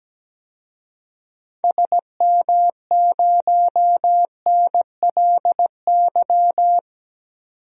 Morse code, heard as a keyed tone, about 17 wpm, 700 Hz. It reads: SM0NLY
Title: DXCC-100 Call Signs - 6 Characters - 17wpm